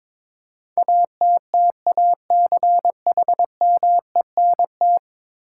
Morse code, heard as a keyed tone, about 22 wpm, 700 Hz. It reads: ATTACHMENT
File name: practice / 22wpm / 10 Letter Words - First 3 Letters Given - 22wpm